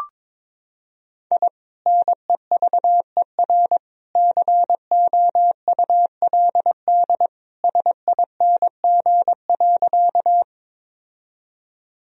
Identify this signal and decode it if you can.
Morse code, heard as a keyed tone, about 21 wpm, 700 Hz. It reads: I NEVER COULD SING.
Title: Sentences from Top 1000 Words - Review Part 2 - 22wpm